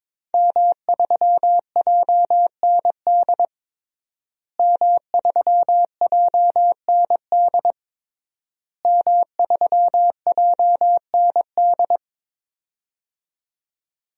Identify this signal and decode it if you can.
Morse code, heard as a keyed tone, about 22 wpm, 700 Hz. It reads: M3JND M3JND M3JND
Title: DXCC-100 Call Signs - 5 Characters - Repeated 3x - 22wpm